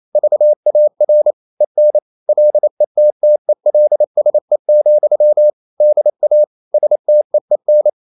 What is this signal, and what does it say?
Morse code, heard as a keyed tone, about 28 wpm, 600 Hz. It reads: VAR EN LETTELSE, DA STEEN
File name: Ferie QRQ 28wpm